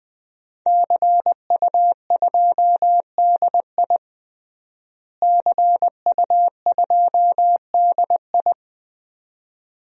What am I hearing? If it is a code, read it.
Morse code, heard as a keyed tone, about 20 wpm, 700 Hz. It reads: CU2DI CU2DI
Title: DXCC-100 Call Signs - 5 Characters - Repeated 2x - 20wpm